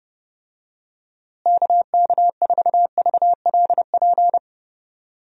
Morse code, heard as a keyed tone, about 31 wpm, 700 Hz. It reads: KK4VLP
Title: USA Call Signs - 6 Characters - Repeated 3x - 30wpm